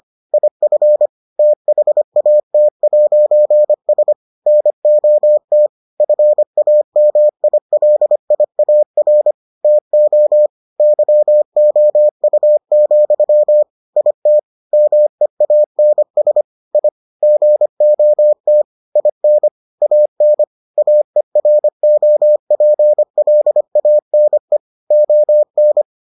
Morse code, heard as a keyed tone, about 25 wpm, 600 Hz. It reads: IF THAT'S NOT FAMILIAR TO YOU, IT MEANS I GOT IN AN AEROPLANE ON